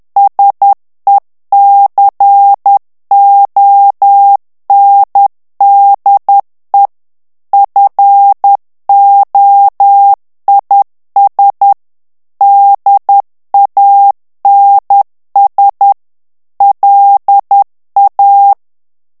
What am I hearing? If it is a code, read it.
Morse code, heard as a keyed tone, about 10 wpm, 800 Hz. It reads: SECONDE FOIS DANS LA
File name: FRANCAIS TEXTE CLAIR 2 VITESSE 10 MOTS MINUTES